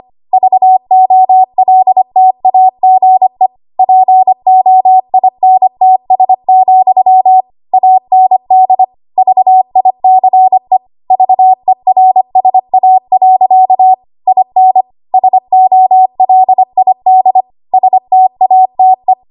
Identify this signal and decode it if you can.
Morse code, heard as a keyed tone, about 25 wpm, 750 Hz. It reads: VOLTAGE POINTS, AND VICE VERSA. IN SOLID STATE
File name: Archive / 25 WPM / 25WPM